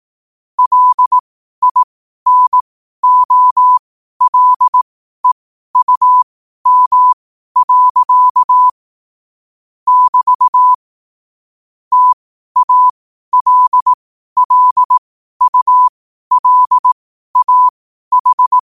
Morse code, heard as a keyed tone, about 18 wpm, spaced as for 12 wpm, 1000 Hz. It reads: LINOLEUM. <BT> TALLULAH
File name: qotdfiles / 12wpm